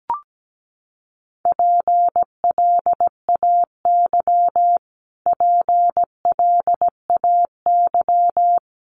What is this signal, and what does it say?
Morse code, heard as a keyed tone, about 17 wpm, 700 Hz. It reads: PLAY PLAY
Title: Top 200 Words Repeated 17wpm